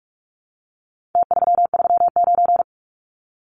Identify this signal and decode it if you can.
Morse code, heard as a keyed tone, about 44 wpm, 700 Hz. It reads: T339
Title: Numbers - Contest Style - 4-Digit 45wpm